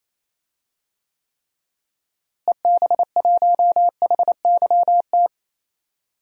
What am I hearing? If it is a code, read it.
Morse code, heard as a keyed tone, about 28 wpm, 700 Hz. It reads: EB1HYT
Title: DXCC-100 Call Signs - 6 Characters - Repeated 3x - 28wpm